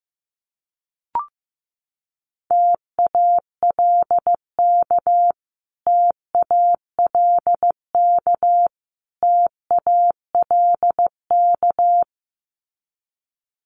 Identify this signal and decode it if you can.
Morse code, heard as a keyed tone, about 15 wpm, 700 Hz. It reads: TALK TALK TALK
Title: Top 500 Words Repeated 15wpm